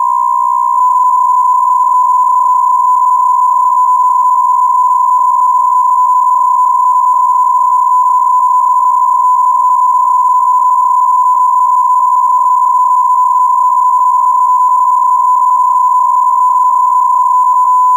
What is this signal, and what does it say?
PSK63, about 62.5 baud.